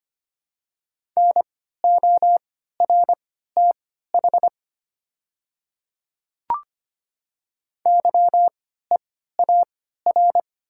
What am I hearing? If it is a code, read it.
Morse code, heard as a keyed tone, about 25 wpm, spaced as for 14 wpm, 700 Hz. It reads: NORTH YEAR
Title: 3x Character Spacing - Top 300 Words - 25wpm